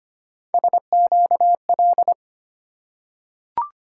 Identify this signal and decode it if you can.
Morse code, heard as a keyed tone, about 25 wpm, 700 Hz. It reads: SQL
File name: Three Letter-Number Combinations - 25wpm